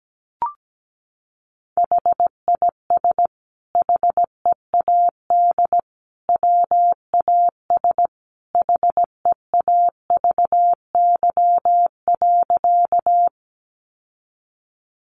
Morse code, heard as a keyed tone, about 17 wpm, 700 Hz. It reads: HIS HEAD WAS HEAVY.
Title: Sentences from Top 1100 Words 17wpm